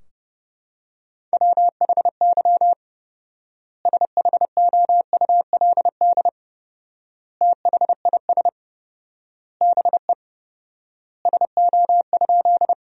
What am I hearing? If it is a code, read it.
Morse code, heard as a keyed tone, about 30 wpm, 700 Hz. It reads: WHY SHOULD THIS BE SO?
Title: 4x Word-Spacing - Sentences from Top 300 Words - 30wpm